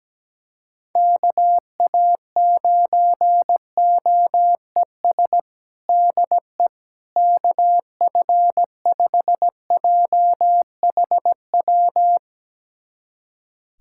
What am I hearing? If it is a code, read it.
Morse code, heard as a keyed tone, about 17 wpm, 700 Hz. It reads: KA9OES DE KF5JHW
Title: USA Callsign DE Callsign 17wpm